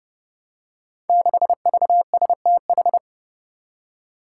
Morse code, heard as a keyed tone, about 29 wpm, 700 Hz. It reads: 6VSTH
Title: Five Challenging Letter-Number Combinations 30wpm